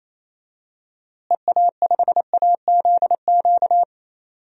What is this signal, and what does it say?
Morse code, heard as a keyed tone, about 28 wpm, 700 Hz. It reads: EA5AZQ
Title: DXCC-100 Call Signs - 6 Characters - Repeated 3x - 28wpm